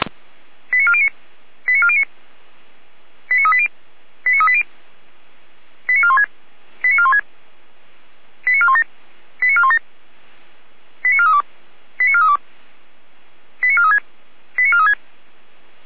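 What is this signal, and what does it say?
Five-tone selective call (ZVEI), heard as quick runs of five tones, tones of about 70 ms. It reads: 89409 89409 89380 89380 89416 89416 89417 89417 89432 89432 89436 89436
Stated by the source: ZVEI five-tone call received by SDR.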